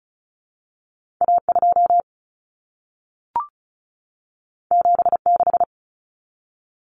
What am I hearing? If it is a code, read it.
Morse code, heard as a keyed tone, about 36 wpm, 700 Hz. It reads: A2 76
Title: Common Cut Numbers - 2-Digit 35wpm